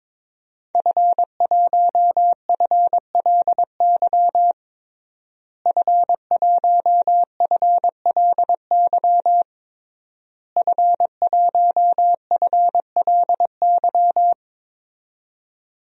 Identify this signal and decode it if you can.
Morse code, heard as a keyed tone, about 22 wpm, 700 Hz. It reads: F1FLY F1FLY F1FLY
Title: DXCC-100 Call Signs - 5 Characters - Repeated 3x - 22wpm